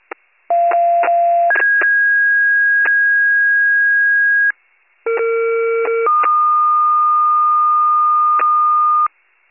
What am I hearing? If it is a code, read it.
Two-tone page, heard as a one-second tone, then a three-second tone, about 690 then 1650 Hz, about 460 then 1220 Hz, with noise. It is heard from 0.5 to 9.1 s.